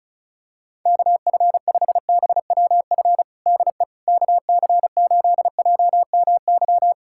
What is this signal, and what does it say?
Morse code, heard as a keyed tone, about 34 wpm, 700 Hz. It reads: KF5BWF DE KC8JMY
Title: USA Callsign DE Callsign - 6 Characters - 35wpm